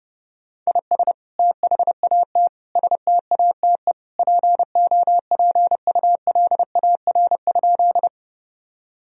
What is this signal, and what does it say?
Morse code, heard as a keyed tone, about 30 wpm, 700 Hz. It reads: IS THAT STATE POPULAR?
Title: Sentences from Top 1000 Words - Review Part 3 - 30wpm